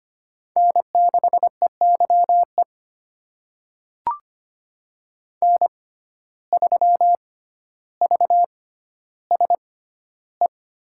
Morse code, heard as a keyed tone, about 25 wpm, spaced as for 13 wpm, 700 Hz. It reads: N6EYE N3VSE
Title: USA Call Signs - 5 Characters - 6x Character Spacing - 25wpm